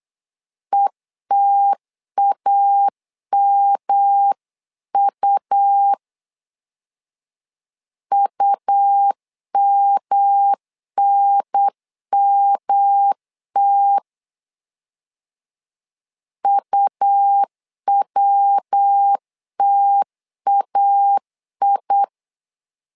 Morse code, heard as a keyed tone, about 8 wpm, 800 Hz. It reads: ETAMU UMNMT UWTAI
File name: LEZIONI / 3 MWRO / 3 MW 8wpm 2501